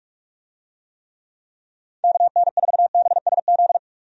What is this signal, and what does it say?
Morse code, heard as a keyed tone, about 41 wpm, 700 Hz. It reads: KN4BSZ
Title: USA Call Signs - 6 Characters - 45wpm